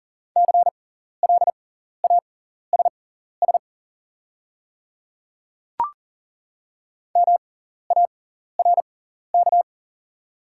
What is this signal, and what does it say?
Morse code, heard as a keyed tone, about 40 wpm, spaced as for 14 wpm, 700 Hz. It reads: CLASS MARK
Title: 6x Character Spacing - Top 500 Words - 40wpm